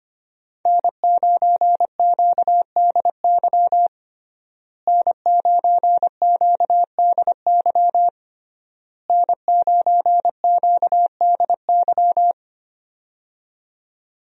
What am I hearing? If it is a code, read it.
Morse code, heard as a keyed tone, about 25 wpm, 700 Hz. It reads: N9QDY N9QDY N9QDY
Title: USA Call Signs - 5 Characters - Repeated 3x - 25wpm